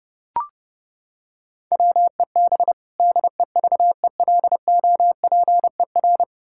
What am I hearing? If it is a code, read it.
Morse code, heard as a keyed tone, about 32 wpm, 700 Hz. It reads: WEB DEVELOPER
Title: Practice QSO - Exchange 3 - More Info - 30wpm